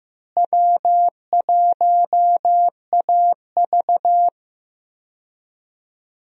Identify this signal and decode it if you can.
Morse code, heard as a keyed tone, about 16 wpm, 700 Hz. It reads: W1AV
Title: USA Call Signs - 4 Characters - Repeated 2x - 15wpm